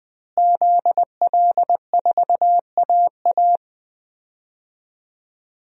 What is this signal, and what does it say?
Morse code, heard as a keyed tone, about 20 wpm, 700 Hz. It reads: ZL4AA
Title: DXCC-100 Call Signs - 5 Characters - Repeated 3x - 20wpm